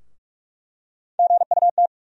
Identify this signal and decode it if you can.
Morse code, heard as a keyed tone, about 44 wpm, 700 Hz. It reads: GUT
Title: USA Call Sign Postfixes - 3 Letters 45wpm